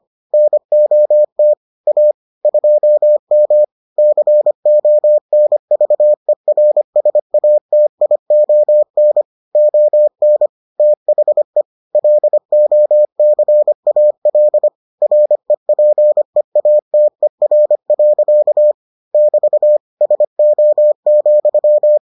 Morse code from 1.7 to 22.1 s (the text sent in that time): A 2M CONVERSATION ON THE LOCAL REPEATER. <BT> SO,